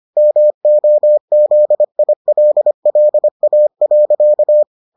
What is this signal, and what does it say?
Morse code, heard as a keyed tone, about 25 wpm, 600 Hz. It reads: MOZILLA.